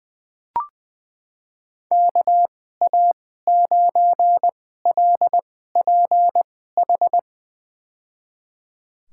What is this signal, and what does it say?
Morse code, heard as a keyed tone, about 20 wpm, spaced as for 14 wpm, 700 Hz. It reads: KA9LPH
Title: USA Call Signs - 6 Characters - 2x Character Spacing - 20wpm